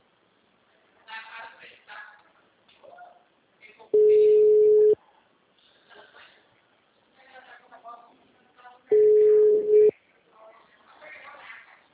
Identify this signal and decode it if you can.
Ringback tone (Europe, 425 Hz), with speech.